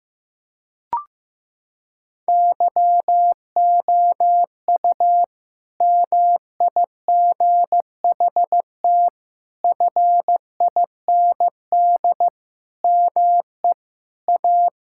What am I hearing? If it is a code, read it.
Morse code, heard as a keyed tone, about 16 wpm, 700 Hz. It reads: YOU MIGHT FIND ME A
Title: Sentences from Top 1600 Words 15wpm